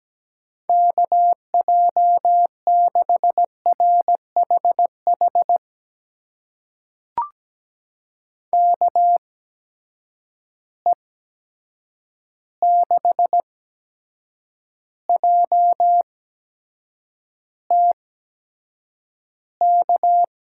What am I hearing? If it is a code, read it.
Morse code, heard as a keyed tone, about 17 wpm, spaced as for 7 wpm, 700 Hz. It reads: KJ6RHH KE6JTK
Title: USA Call Signs - 6 Characters - 8x Character Spacing - 17wpm